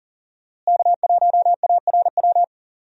Morse code, heard as a keyed tone, about 39 wpm, 700 Hz. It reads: K1ARW
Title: USA Call Signs - 5 Characters - Repeated 3x - 40wpm